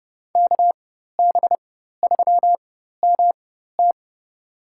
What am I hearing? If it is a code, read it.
Morse code, heard as a keyed tone, about 30 wpm, spaced as for 14 wpm, 700 Hz. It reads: KB3MT
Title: USA Call Signs - 5 Characters - 4x Character Spacing - 30wpm